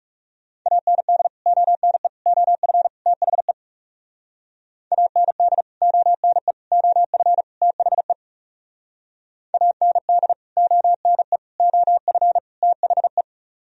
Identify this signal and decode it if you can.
Morse code, heard as a keyed tone, about 37 wpm, 700 Hz. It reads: AND ONE OF THE AND ONE OF THE AND ONE OF THE
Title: Speed-Racing - Sets of 4 Words 30wpm